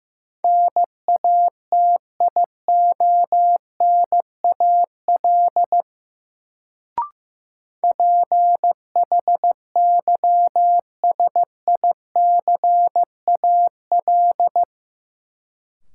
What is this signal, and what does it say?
Morse code, heard as a keyed tone, about 15 wpm, 700 Hz. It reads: NATIONAL PHYSICAL
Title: Sentences from Top 1000 Words - Review Part 4 - New Word Drill - 15wpm